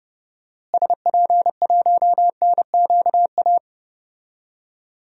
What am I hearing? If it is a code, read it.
Morse code, heard as a keyed tone, about 30 wpm, 700 Hz. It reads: SP1NQA
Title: DXCC-100 Call Signs - 6 Characters - Repeated 3x - 30wpm